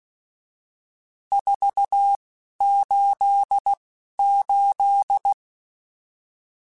Morse code, heard as a keyed tone, about 16 wpm, spaced as for 11 wpm, 800 Hz. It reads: 488
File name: RSTs / RST-mp3s / Various RST @10